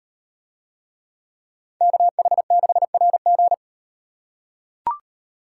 Morse code, heard as a keyed tone, about 35 wpm, 700 Hz. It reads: KH6RG